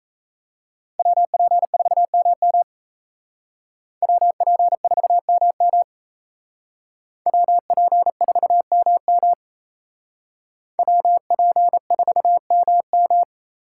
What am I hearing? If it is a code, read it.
Morse code, heard as a keyed tone, about 35 wpm, 700 Hz. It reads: WP4MM WP4MM WP4MM WP4MM